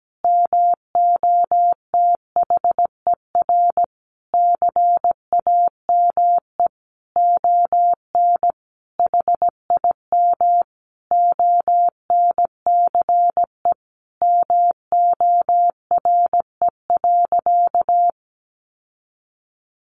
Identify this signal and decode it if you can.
Morse code, heard as a keyed tone, about 17 wpm, 700 Hz. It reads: MOTHER CAME ON HIM ONCE MORE.
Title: Sentences from Top 1100 Words 17wpm